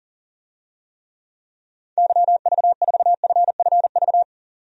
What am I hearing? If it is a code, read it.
Morse code, heard as a keyed tone, about 40 wpm, 700 Hz. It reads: YV4FFV
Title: DXCC-100 Call Signs - 6 Characters - 40wpm